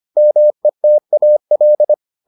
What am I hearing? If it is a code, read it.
Morse code, heard as a keyed tone, about 25 wpm, 600 Hz. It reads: METAL